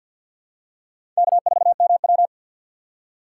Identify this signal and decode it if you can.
Morse code, heard as a keyed tone, about 48 wpm, 700 Hz. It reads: K4MW
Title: USA Call Signs - 4 Characters - 3x Character Spacing - 50wpm